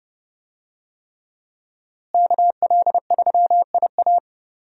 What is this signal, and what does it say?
Morse code, heard as a keyed tone, about 30 wpm, 700 Hz. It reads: KL3IA
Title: USA Call Signs - 5 Characters - Repeated 3x - 30wpm